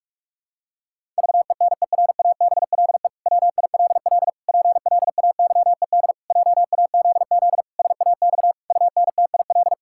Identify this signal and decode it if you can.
Morse code, heard as a keyed tone, about 46 wpm, 700 Hz. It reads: VENERABLE WILL PLAYED JAZZ SAX UNTIL
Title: Pangram Challenge v2 - 45wpm